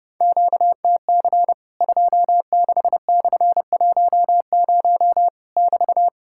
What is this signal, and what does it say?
Morse code, heard as a keyed tone, about 30 wpm, 700 Hz. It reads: QTC 26/10 <BT>
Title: Worked All Europe DX Contest - QTC Exchange - 30wpm